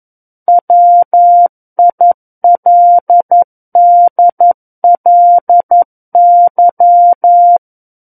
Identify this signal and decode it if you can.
Morse code, heard as a keyed tone, about 11 wpm, 700 Hz. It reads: WILDLY
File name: Book One / 02-Chapter 2 - 11 WPM